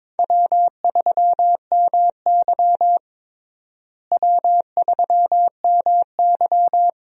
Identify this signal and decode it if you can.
Morse code, heard as a keyed tone, about 22 wpm, 700 Hz. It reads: W3MY W3MY
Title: USA Call Signs - 4 Characters - Repeated 3x - 22wpm